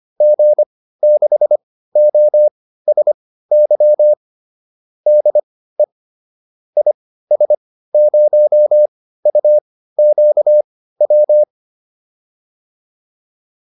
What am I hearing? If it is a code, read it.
Morse code, heard as a keyed tone, about 25 wpm, spaced as for 15 wpm, 600 Hz. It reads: G6OSY DE IS0UQW
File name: QSO409 15